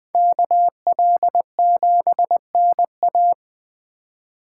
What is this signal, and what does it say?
Morse code, heard as a keyed tone, about 24 wpm, 700 Hz. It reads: KL7NA
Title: Speed Racing - USA 2x2 Call Signs 20wpm